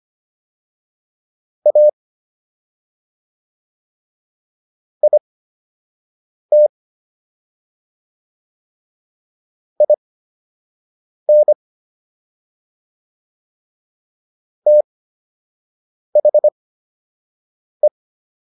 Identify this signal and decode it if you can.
Morse code, heard as a keyed tone, about 25 wpm, spaced as for 6 wpm, 600 Hz. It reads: A IT IN THE